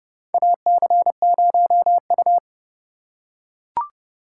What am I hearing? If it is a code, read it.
Morse code, heard as a keyed tone, about 32 wpm, 700 Hz. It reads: AC0U
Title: USA Call Signs - 4 Characters - 4x Character Spacing - 30wpm